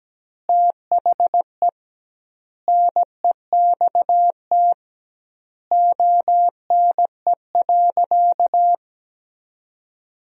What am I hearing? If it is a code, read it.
Morse code, heard as a keyed tone, about 17 wpm, 700 Hz. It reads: THE NEXT ONE.
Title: 2x Word-Spacing - Sentences from Top 400 Words - 17wpm